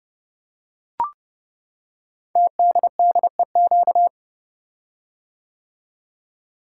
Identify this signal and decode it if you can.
Morse code, heard as a keyed tone, about 30 wpm, 700 Hz. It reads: TDDEQ
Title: Code Groups - 5 Letters - 30wpm